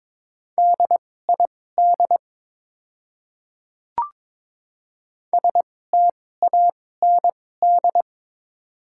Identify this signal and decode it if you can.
Morse code, heard as a keyed tone, about 22 wpm, spaced as for 16 wpm, 700 Hz. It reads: DID STAND
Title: 2x Character Spacing - Top 300 Words - 22wpm